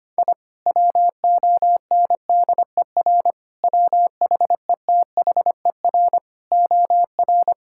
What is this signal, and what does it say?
Morse code, heard as a keyed tone, about 25 wpm, 700 Hz. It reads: I WONDER WHETHER OR
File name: practice / 25wpm / Sentences with 3-Word Phrases - 25wpm